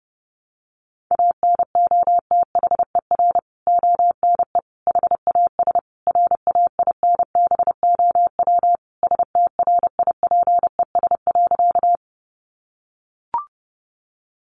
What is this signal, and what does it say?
Morse code, heard as a keyed tone, about 30 wpm, 700 Hz. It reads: ANOTHER ONE HAS RAINBOW STRIPES.
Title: Additional Easy Phrases 30wpm